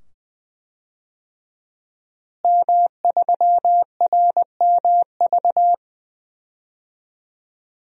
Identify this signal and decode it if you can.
Morse code, heard as a keyed tone, about 20 wpm, 700 Hz. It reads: M3RMV
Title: DXCC-100 Call Signs - 5 Characters - Repeated 2x - 20wpm